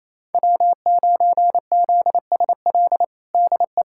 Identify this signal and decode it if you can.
Morse code, heard as a keyed tone, about 28 wpm, 700 Hz. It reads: W9ZSL DE
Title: USA Callsign DE Callsign - 5 Characters - 28wpm